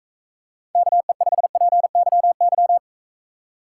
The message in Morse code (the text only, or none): KE5PYY